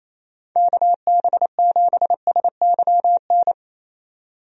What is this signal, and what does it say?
Morse code, heard as a keyed tone, about 28 wpm, 700 Hz. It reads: KB7SYN
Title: USA Call Signs - 6 Characters - 28wpm